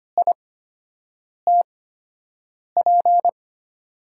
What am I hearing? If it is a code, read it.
Morse code, heard as a keyed tone, about 25 wpm, spaced as for 7 wpm, 700 Hz. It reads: ITP
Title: USA Call Signs - 6 Characters - 8x Character Spacing - 25wpm